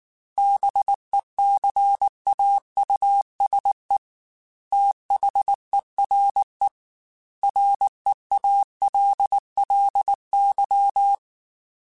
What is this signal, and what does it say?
Morse code, heard as a keyed tone, about 19 wpm, 800 Hz. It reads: BECAUSE THERE REALLY
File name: Technical Subjects / Technical Subjects-mp3s / Gates Intro @18